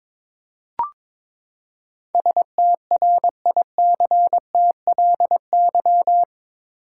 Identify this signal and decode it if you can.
Morse code, heard as a keyed tone, about 22 wpm, 700 Hz. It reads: STRICTLY
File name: practice / 22wpm / Eight Letter Words - 22wpm